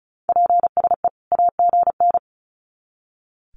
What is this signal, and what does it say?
Morse code, heard as a keyed tone, about 34 wpm, 700 Hz. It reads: PSE AGN
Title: Common QSO Phrases 35wpm